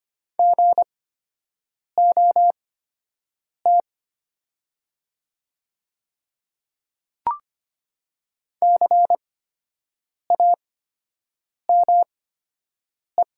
Morse code, heard as a keyed tone, about 25 wpm, spaced as for 7 wpm, 700 Hz. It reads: GOT CAME
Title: 8x Character Spacing - Top 300 Words - 25wpm